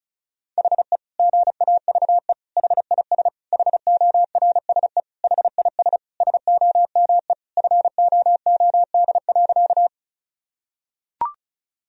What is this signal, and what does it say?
Morse code, heard as a keyed tone, about 34 wpm, 700 Hz. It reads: HE GAVE HIS HORSE HIS SOME FOOD.
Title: Sentences from Top 1000 Words - Review Part 1 - 35wpm